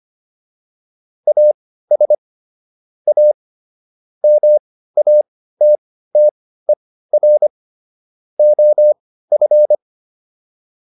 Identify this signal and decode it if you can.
Morse code, heard as a keyed tone, about 25 wpm, spaced as for 15 wpm, 600 Hz. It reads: AS A MATTER OF